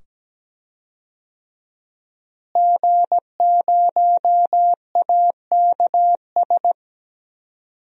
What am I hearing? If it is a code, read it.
Morse code, heard as a keyed tone, about 17 wpm, 700 Hz. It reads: G0AKS
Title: DXCC-100 Call Signs - 5 Characters - Repeated 2x - 17wpm